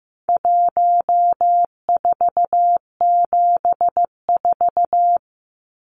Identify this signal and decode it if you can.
Morse code, heard as a keyed tone, about 15 wpm, 700 Hz. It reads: 1474
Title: Numbers - 4-Digit 15wpm